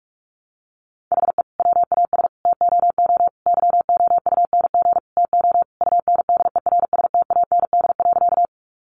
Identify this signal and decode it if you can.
Morse code, heard as a keyed tone, about 44 wpm, 700 Hz. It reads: HE WAS TOO YOUNG TO UNDERSTAND.
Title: Sentences from Top 600 Words 45wpm